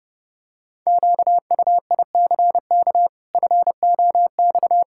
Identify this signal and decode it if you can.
Morse code, heard as a keyed tone, about 32 wpm, 700 Hz. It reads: QUICK FOX
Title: Pangram Challenge v2 - 30wpm